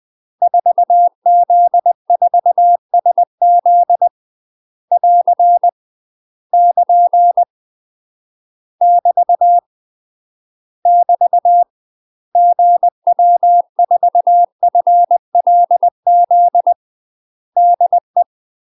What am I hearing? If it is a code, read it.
Morse code, heard as a keyed tone, about 20 wpm, 700 Hz. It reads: 4Z4SZ <AR> <KN> <BT> <BT> GW4FLZ DE
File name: UK MASTER QSOs TRAINING / 20WPM / UK MASTER QSOs 0034